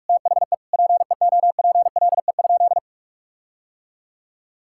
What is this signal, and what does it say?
Morse code, heard as a keyed tone, about 46 wpm, 700 Hz. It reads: THE PEOPLE?
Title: Sentences from Top 1600 Words 45wpm